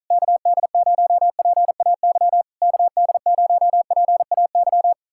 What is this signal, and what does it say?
Morse code, heard as a keyed tone, about 46 wpm, 700 Hz. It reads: KD0PAY KD0PAY